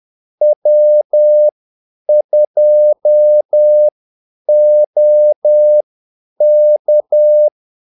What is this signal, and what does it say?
Morse code, heard as a keyed tone, about 10 wpm, spaced as for 8 wpm, 600 Hz. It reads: W2OK
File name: Callsigns-10-8wpm